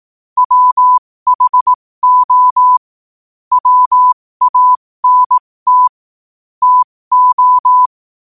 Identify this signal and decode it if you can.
Morse code, heard as a keyed tone, about 18 wpm, spaced as for 15 wpm, 1000 Hz. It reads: WHO WANT TO
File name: qotdfiles / 15wpm